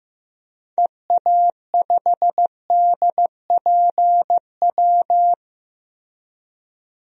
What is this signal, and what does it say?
Morse code, heard as a keyed tone, about 15 wpm, 700 Hz. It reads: EA5DPW
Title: DXCC-100 Call Signs - 6 Characters - 15wpm